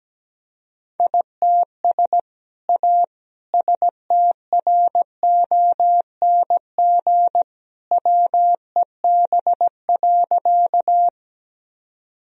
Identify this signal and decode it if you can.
Morse code, heard as a keyed tone, about 17 wpm, 700 Hz. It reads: ITS A STRONG WEB.
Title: Sentences from Top 1000 Words - Review Part 4 - 17wpm